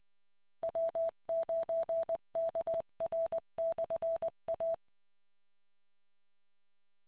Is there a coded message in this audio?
Yes: Morse code from 0.6 to 4.7 s, heard as a keyed tone, about 20 wpm, 660 Hz. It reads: W9KR/A